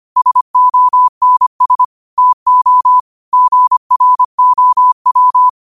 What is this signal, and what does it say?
Morse code, heard as a keyed tone, about 25 wpm, 1000 Hz. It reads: SONS TO GROW
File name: qotdfiles / 25wpm